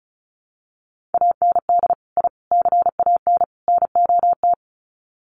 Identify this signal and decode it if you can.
Morse code, heard as a keyed tone, about 34 wpm, 700 Hz. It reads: AND I CAN NOT
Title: Sets of 4 Words 35wpm